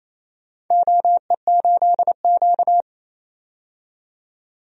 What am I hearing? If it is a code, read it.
Morse code, heard as a keyed tone, about 28 wpm, 700 Hz. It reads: OE8Q
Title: DXCC-100 Call Signs - 4 Characters - 28wpm